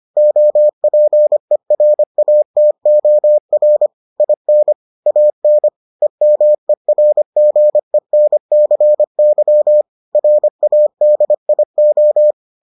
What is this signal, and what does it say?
Morse code, heard as a keyed tone, about 25 wpm, 600 Hz. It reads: OPERATOR IN AN EMERGENCY RADIO